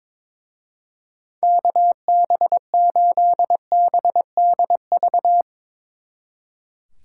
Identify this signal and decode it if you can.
Morse code, heard as a keyed tone, about 24 wpm, 700 Hz. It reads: KB8BDV
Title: Speed Racing - USA 2x3 Call Signs 22wpm